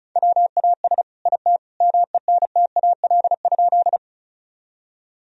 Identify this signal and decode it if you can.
Morse code, heard as a keyed tone, about 34 wpm, 700 Hz. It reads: WAS IT MENTAL?
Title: Sentences from Top 1000 Words - Review Part 4 - 35wpm